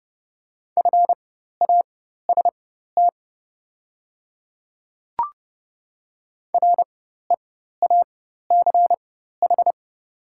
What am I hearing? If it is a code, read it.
Morse code, heard as a keyed tone, about 30 wpm, spaced as for 14 wpm, 690 Hz. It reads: FAST REACH
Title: 4x Character Spacing - Top 500 Words - 30wpm